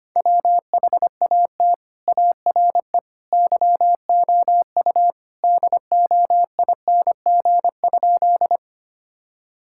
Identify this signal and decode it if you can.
Morse code, heard as a keyed tone, about 25 wpm, 700 Hz. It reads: WHAT ARE YOU DOING?
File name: practice / 25wpm / Sentences with 3-Word Phrases - 25wpm